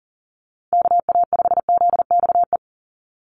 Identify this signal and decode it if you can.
Morse code, heard as a keyed tone, about 40 wpm, 700 Hz. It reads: KA5ZXE
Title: USA Call Signs 40wpm